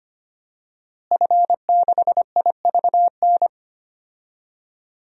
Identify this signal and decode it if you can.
Morse code, heard as a keyed tone, about 25 wpm, 700 Hz. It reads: F6IVN
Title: DXCC-100 Call Signs - 5 Characters - Repeated 3x - 25wpm